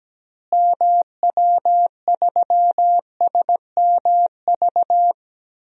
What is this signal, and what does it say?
Morse code, heard as a keyed tone, about 17 wpm, 700 Hz. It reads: MW3SMV
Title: DXCC-100 Call Signs - 6 Characters - Repeated 2x - 17wpm